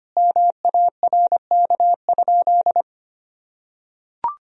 Morse code, heard as a keyed tone, about 25 wpm, 700 Hz. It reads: MARK?
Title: Sentences from Top 1000 Words - Review Part 1 - 25wpm